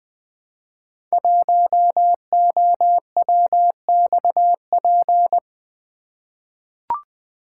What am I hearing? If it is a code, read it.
Morse code, heard as a keyed tone, about 20 wpm, 700 Hz. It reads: 1OWXP
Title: Code Groups - 5 Letter:Numbers - 20wpm